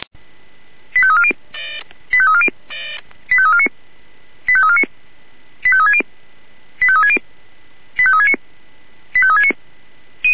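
Five-tone selective call (ZVEI), heard as quick runs of five tones, tones of about 70 ms. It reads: 85340 85340 85349 85349 85360 85360 85369 85369
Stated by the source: Five-tone ZVEI selective call received on an SDR.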